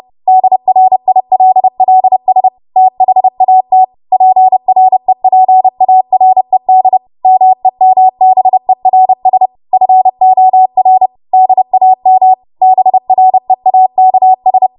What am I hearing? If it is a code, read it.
Morse code, heard as a keyed tone, about 30 wpm, 750 Hz. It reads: DRILLS THAT PREPARED MEMBERS FOR DAM BREAKS
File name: Morse / 30WPM